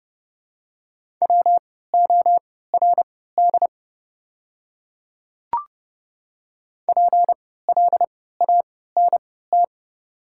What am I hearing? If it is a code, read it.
Morse code, heard as a keyed tone, about 30 wpm, spaced as for 17 wpm, 700 Hz. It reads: WORD PLANT
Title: 3x Character Spacing - Top 300 Words - 30wpm